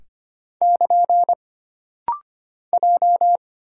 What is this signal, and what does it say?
Morse code, heard as a keyed tone, about 25 wpm, 700 Hz. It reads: <KN> J
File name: Single Character (Letter:Number:Punctuation:Prosign) - Rapid-Fire - 25wpm